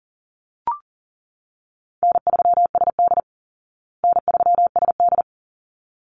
Morse code, heard as a keyed tone, about 37 wpm, 700 Hz. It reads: N3SD N3SD
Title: Easy Call Signs Repeated 2x - 40wpm